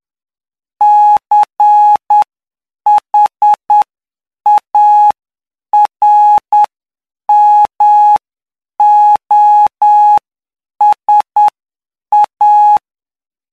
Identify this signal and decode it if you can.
Morse code, heard as a keyed tone, about 9 wpm, spaced as for 7 wpm, 840 Hz. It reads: CHARMOSA
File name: cw / py2lcd / 09-PY2LCD-Li��o 18